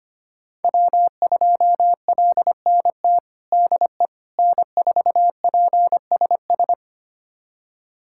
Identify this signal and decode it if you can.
Morse code, heard as a keyed tone, about 25 wpm, 700 Hz. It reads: W2LNT DE N4PSS
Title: USA Callsign DE Callsign - 5 Characters - 25wpm